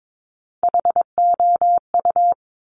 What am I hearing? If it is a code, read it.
Morse code, heard as a keyed tone, about 22 wpm, 700 Hz. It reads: HOU
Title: Top Three Letter Combinations in Words 22wpm